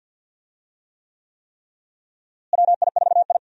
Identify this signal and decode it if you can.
Morse code, heard as a keyed tone, about 52 wpm, 700 Hz. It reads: WI4I
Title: USA Call Signs - 4 Characters - Repeated 3x - 50wpm